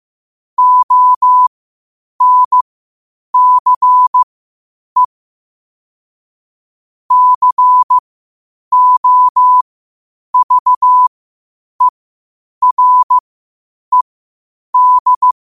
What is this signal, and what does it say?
Morse code, heard as a keyed tone, about 15 wpm, spaced as for 8 wpm, 1000 Hz. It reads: ONCE COVERED
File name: qotdfiles / 8wpm